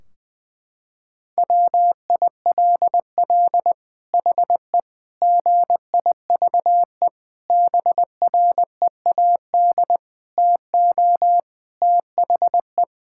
Morse code, heard as a keyed tone, about 20 wpm, 700 Hz. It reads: WILL HE GIVE BREAD TO THE
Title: Sentences from Top 1600 Words 20wpm